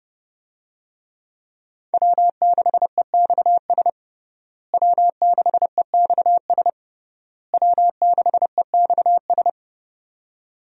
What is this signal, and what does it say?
Morse code, heard as a keyed tone, about 30 wpm, 700 Hz. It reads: W6EXS W6EXS W6EXS
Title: USA Call Signs - 5 Characters - Repeated 3x - 30wpm